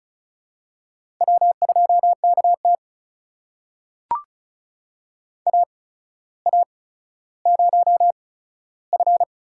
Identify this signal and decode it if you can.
Morse code, heard as a keyed tone, about 36 wpm, spaced as for 15 wpm, 700 Hz. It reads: W2KT AA0F
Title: USA Call Signs - 4 Characters - 8x Character Spacing - 35wpm